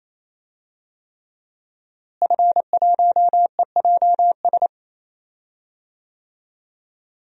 Morse code, heard as a keyed tone, about 28 wpm, 700 Hz. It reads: F1EJS
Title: DXCC-100 Call Signs - 5 Characters - Repeated 3x - 28wpm